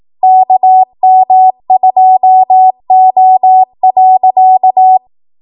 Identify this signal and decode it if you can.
Morse code, heard as a keyed tone, about 18 wpm, 750 Hz. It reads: KM2O.